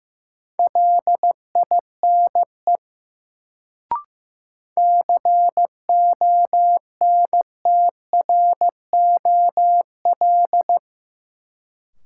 Morse code, heard as a keyed tone, about 15 wpm, 700 Hz. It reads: LINE CONTROL
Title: Sentences from Top 1200 Words - New Word Drill Alt - 15wpm